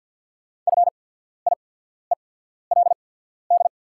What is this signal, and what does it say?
Morse code, heard as a keyed tone, about 48 wpm, spaced as for 14 wpm, 690 Hz. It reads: FIELD
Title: 8x Character Spacing - Top 500 Words - 50wpm